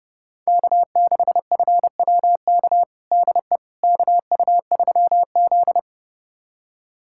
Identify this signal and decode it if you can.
Morse code, heard as a keyed tone, about 29 wpm, 700 Hz. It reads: K6FWK DE KU3Z
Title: USA Callsign DE Callsign 30wpm